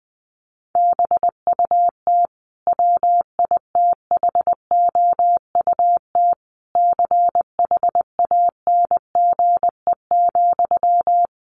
Morse code, heard as a keyed tone, about 19 wpm, 700 Hz. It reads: BUT WITHOUT CHANGE,
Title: Sentences from Top 1400 Words 20wpm